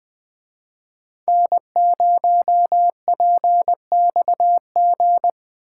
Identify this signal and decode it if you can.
Morse code, heard as a keyed tone, about 21 wpm, 700 Hz. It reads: N0PXG
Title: Speed Racing - USA 1x3 Call Signs 17wpm